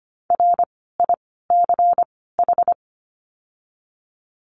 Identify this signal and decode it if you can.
Morse code, heard as a keyed tone, about 25 wpm, 700 Hz. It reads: R I C H
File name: course / 25:10wpm / 094 Lesson 16f - Letter c - Words 25wpm